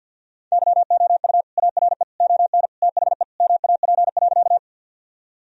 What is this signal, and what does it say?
Morse code, heard as a keyed tone, about 50 wpm, 700 Hz. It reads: YOU ARE ON THE MAP.
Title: Sentences from Top 1000 Words - Review Part 2 - 50wpm